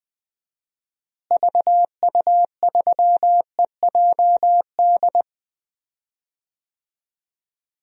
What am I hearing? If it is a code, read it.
Morse code, heard as a keyed tone, about 20 wpm, 700 Hz. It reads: VU3EJD
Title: DXCC-100 Call Signs - 6 Characters - Repeated 3x - 20wpm